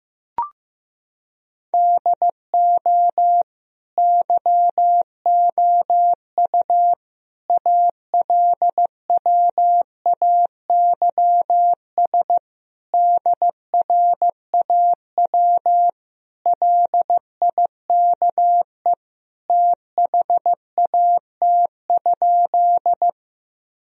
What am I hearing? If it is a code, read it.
Morse code, heard as a keyed tone, about 15 wpm, 700 Hz. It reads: DO YOU ALWAYS DRAW LIKE THAT?
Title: Sentences from Top 1000 Words - Review Part 1 - 15wpm